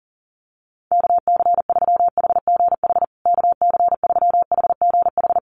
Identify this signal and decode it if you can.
Morse code, heard as a keyed tone, about 40 wpm, 700 Hz. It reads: KC3HGH KC3HGH
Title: USA Call Signs Repeated 2x 40wpm